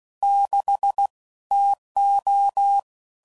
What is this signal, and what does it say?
Morse code, heard as a keyed tone, about 16 wpm, 800 Hz. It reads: 6 TO
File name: CW Code Groups @15WPM / Weather Sets / Weather 4 24